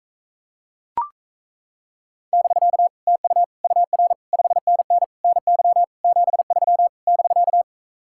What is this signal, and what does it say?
Morse code, heard as a keyed tone, about 36 wpm, 700 Hz. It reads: <BK> TU UR 5NN NY 73 <BK>